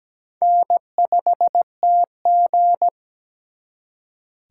Morse code, heard as a keyed tone, about 20 wpm, 700 Hz. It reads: N5TG
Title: Speed Racing - USA 1x2 Call Signs 17wpm